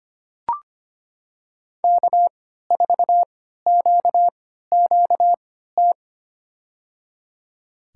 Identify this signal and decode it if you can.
Morse code, heard as a keyed tone, about 25 wpm, spaced as for 14 wpm, 700 Hz. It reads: K4QQT
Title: USA Call Signs - 5 Characters - 3x Character Spacing - 25wpm